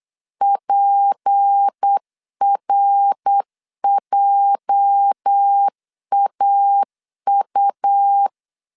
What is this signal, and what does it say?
Morse code, heard as a keyed tone, about 8 wpm, 800 Hz. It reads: PRJAU
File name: LEZIONI / 8 YJ45 / 8 YJ 8wpm 3313